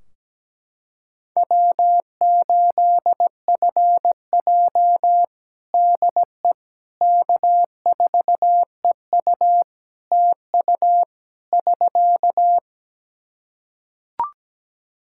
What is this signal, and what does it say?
Morse code, heard as a keyed tone, about 17 wpm, 700 Hz. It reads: W8FJ DE K4EU TU <SK>
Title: Practice QSO - Exchange 4 - Ending - 17wpm